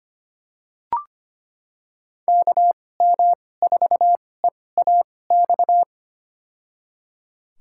Morse code, heard as a keyed tone, about 25 wpm, spaced as for 18 wpm, 700 Hz. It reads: KM4EAX
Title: USA Call Signs - 2x Character Spacing - 25wpm